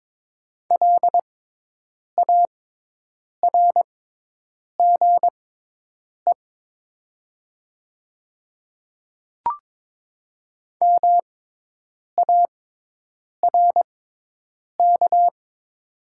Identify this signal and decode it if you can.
Morse code, heard as a keyed tone, about 22 wpm, spaced as for 8 wpm, 700 Hz. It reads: LARGE MARK
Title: 6x Character Spacing - Top 500 Words - 22wpm